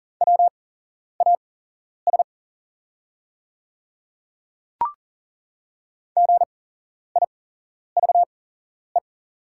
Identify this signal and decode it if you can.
Morse code, heard as a keyed tone, about 39 wpm, spaced as for 11 wpm, 690 Hz. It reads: WAS GIVE
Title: 8x Character Spacing - Top 200 Words - 40wpm